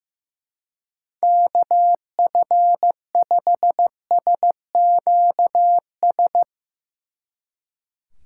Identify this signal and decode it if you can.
Morse code, heard as a keyed tone, about 16 wpm, 700 Hz. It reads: KF5SQS
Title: Speed Racing - USA 2x3 Call Signs 15wpm